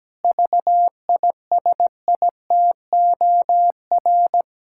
Morse code, heard as a keyed tone, about 17 wpm, 700 Hz. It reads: VISITOR
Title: Pangram Challenge v2 - 17wpm